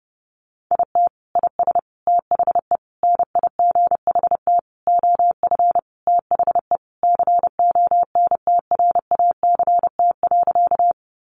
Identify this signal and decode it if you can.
Morse code, heard as a keyed tone, about 30 wpm, 700 Hz. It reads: IT IS THE NIGHT OF THE CONTRACT.
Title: Sentences from Top 1400 Words 30wpm